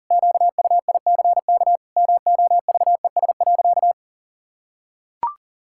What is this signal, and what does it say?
Morse code, heard as a keyed tone, about 40 wpm, 700 Hz. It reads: QUICK MOVES.
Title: Pangram Challenge v2 - 40wpm